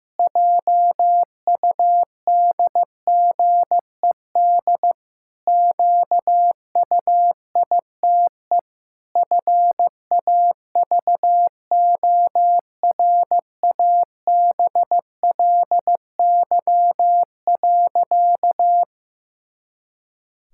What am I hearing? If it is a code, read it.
Morse code, heard as a keyed tone, about 15 wpm, 700 Hz. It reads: JUDGED QUITE FAVORABLY.
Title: Pangram Challenge v2 - 15wpm